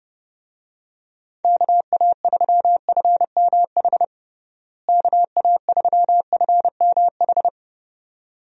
Morse code, heard as a keyed tone, about 30 wpm, 700 Hz. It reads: KA3FMH KA3FMH
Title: USA Call Signs - 6 Characters - Repeated 2x - 30wpm